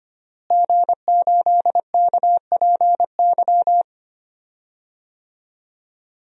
Morse code, heard as a keyed tone, about 25 wpm, 700 Hz. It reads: G8KPY
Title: DXCC-100 Call Signs - 5 Characters - Repeated 2x - 25wpm